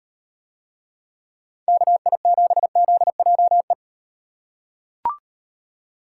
Morse code, heard as a keyed tone, about 38 wpm, 700 Hz. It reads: KI7ZJE